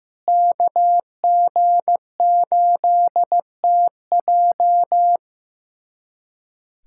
Morse code, heard as a keyed tone, about 16 wpm, 700 Hz. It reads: KG8TJ
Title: USA Callsign DE Callsign - 5 Characters - 15wpm